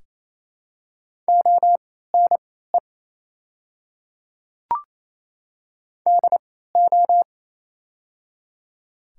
Morse code, heard as a keyed tone, about 28 wpm, spaced as for 16 wpm, 700 Hz. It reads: ONE DO
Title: 3x Character Spacing - Top 200 Words - 28wpm